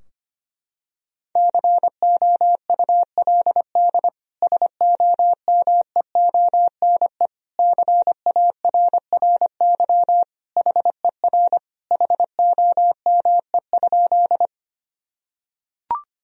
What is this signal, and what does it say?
Morse code, heard as a keyed tone, about 25 wpm, 700 Hz. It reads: COULD SOMEONE CARRY HER HOME?
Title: Sentences from Top 1000 Words - Review Part 4 - 25wpm